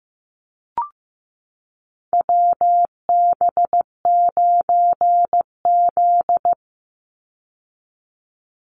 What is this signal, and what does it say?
Morse code, heard as a keyed tone, about 15 wpm, 700 Hz. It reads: WB9Z
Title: Call Signs Repeated 2x - 15wpm